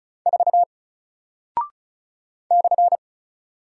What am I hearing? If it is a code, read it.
Morse code, heard as a keyed tone, about 36 wpm, 700 Hz. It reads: 4 /
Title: Single Character (Letter:Number:Punctuation:Prosign) Alt 35wpm